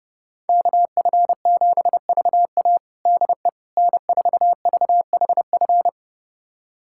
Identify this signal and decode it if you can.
Morse code, heard as a keyed tone, about 31 wpm, 700 Hz. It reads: KF7VA DE N4VHF
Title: USA Callsign DE Callsign - 5 Characters - 30wpm